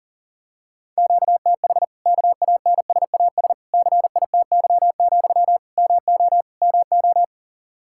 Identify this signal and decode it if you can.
Morse code, heard as a keyed tone, about 40 wpm, 700 Hz. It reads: QTH KANSAS CITY, MO MO
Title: Practice QSO - Exchange 1 - Beginning - 40wpm